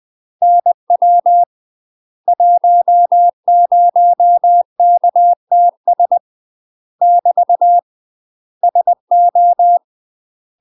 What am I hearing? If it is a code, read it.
Morse code, heard as a keyed tone, about 20 wpm, 700 Hz. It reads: NW 10KTS <BT> SO
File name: UK MASTER QSOs TRAINING / 20WPM / UK MASTER QSOs 0026